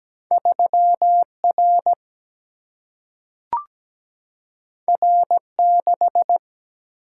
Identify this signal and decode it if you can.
Morse code, heard as a keyed tone, about 17 wpm, 700 Hz. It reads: 3R R6
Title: Single-Letter Single-Number 17wpm